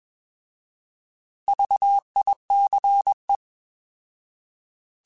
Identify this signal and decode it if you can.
Morse code, heard as a keyed tone, about 21 wpm, 800 Hz. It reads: VICE
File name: Alphabetical / Alphabetical mp3s / Vs @20